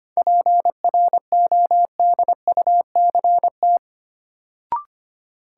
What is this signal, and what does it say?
Morse code, heard as a keyed tone, about 25 wpm, 700 Hz. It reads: PRODUCT
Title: Sentences from Top 1000 Words - Review Part 2 - New Word Drill - 25wpm